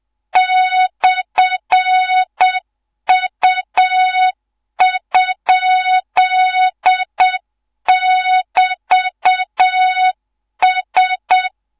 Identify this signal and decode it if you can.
Morse code, heard as a keyed tone, about 7 wpm, 3700 Hz. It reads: /U?<BT>S